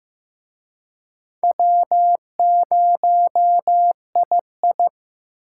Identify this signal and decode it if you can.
Morse code, heard as a keyed tone, about 18 wpm, 700 Hz. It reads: W0II
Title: Speed Racing - USA 1x2 Call Signs 15wpm